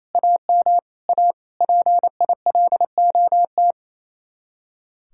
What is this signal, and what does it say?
Morse code, heard as a keyed tone, about 32 wpm, 700 Hz. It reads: AM A PILOT
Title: Speed-Racing Common QSO Phrases - 28wpm